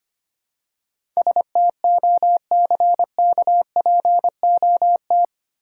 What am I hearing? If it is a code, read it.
Morse code, heard as a keyed tone, about 28 wpm, 700 Hz. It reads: STOCKPOT
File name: practice / 25wpm / Difficult Words - Speed-Racing with Hints - 25wpm